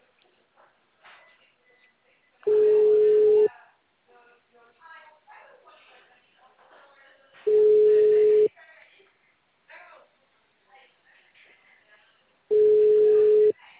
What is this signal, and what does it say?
Ringback tone (Europe, 425 Hz), with speech.